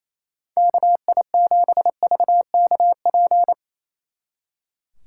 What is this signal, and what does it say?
Morse code, heard as a keyed tone, about 35 wpm, 700 Hz. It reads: KI7VKP
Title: Speed Racing - USA 2x3 Call Signs 28wpm